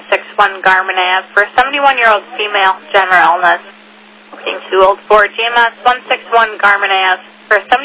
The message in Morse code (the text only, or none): none